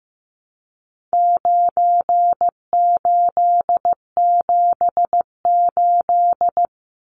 Morse code, heard as a keyed tone, about 15 wpm, 700 Hz. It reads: 9878
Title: Numbers - 4-Digit 15wpm